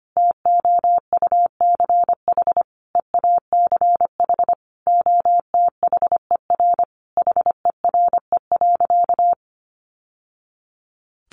Morse code, heard as a keyed tone, about 25 wpm, 700 Hz. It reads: TOUCH EACH OTHER HERE.
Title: Sentences from Top 1200 Words 25wpm